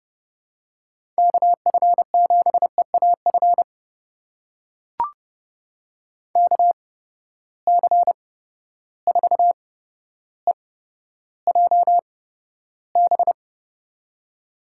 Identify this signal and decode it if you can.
Morse code, heard as a keyed tone, about 30 wpm, spaced as for 13 wpm, 700 Hz. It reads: KF7EAF KC4EJB
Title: USA Call Signs - 6 Characters - 8x Character Spacing - 30wpm